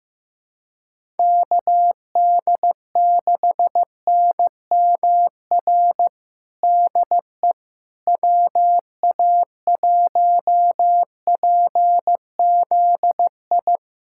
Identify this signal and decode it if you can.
Morse code, heard as a keyed tone, about 15 wpm, 700 Hz. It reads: KD6NMR DE WA1PZI
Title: USA Callsign DE Callsign 15wpm